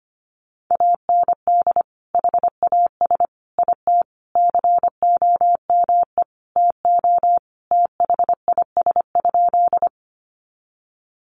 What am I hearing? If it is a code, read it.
Morse code, heard as a keyed tone, about 25 wpm, 700 Hz. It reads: AND HAS IT COME TO THIS?
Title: Sentences from Top 100 Words 25wpm